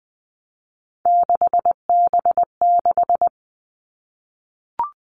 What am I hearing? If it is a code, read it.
Morse code, heard as a keyed tone, about 20 wpm, 700 Hz. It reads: 6B6
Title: All Cut Numbers - 3-Digit 20wpm